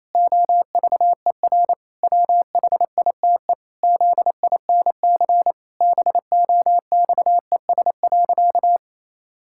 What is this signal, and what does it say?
Morse code, heard as a keyed tone, about 28 wpm, 700 Hz. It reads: OVER WHITE ZINC BOXES.
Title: Pangram Challenge v2 - 28wpm